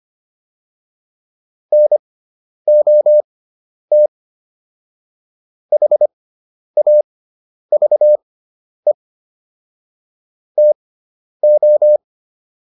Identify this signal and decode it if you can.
Morse code, heard as a keyed tone, about 25 wpm, spaced as for 10 wpm, 600 Hz. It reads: NOT HAVE TO
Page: intermediate-practice-files